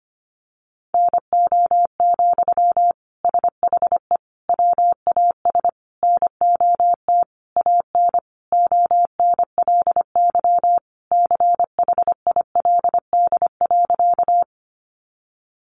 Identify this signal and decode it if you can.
Morse code, heard as a keyed tone, about 25 wpm, 700 Hz. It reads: NO, SHE WAS NOT AN ONLY CHILD.
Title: Sentences from Top 800 Words 25wpm